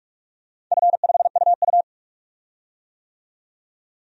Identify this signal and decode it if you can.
Morse code, heard as a keyed tone, about 46 wpm, 700 Hz. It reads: F5UU
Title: DXCC-100 Call Signs - 4 Characters - 45wpm